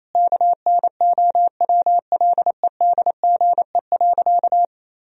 Morse code, heard as a keyed tone, about 28 wpm, 700 Hz. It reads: KNOWLEDGE.
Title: Sentences from Top 1000 Words - Review Part 3 - 28wpm